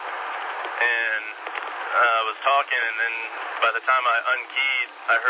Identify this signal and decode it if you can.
Emergency Alert System (SAME) tones, heard as harsh buzzing bursts, with speech. It is heard from 1.0 to 4.9 s.